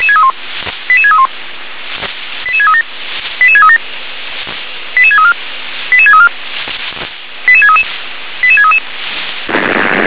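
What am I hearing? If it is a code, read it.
Five-tone selective call (ZVEI), heard as quick runs of five tones, tones of about 70 ms. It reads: 88631 88637 88637 88634 88634 88633 88633
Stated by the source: ZVEI five-tone selective call received by SDR.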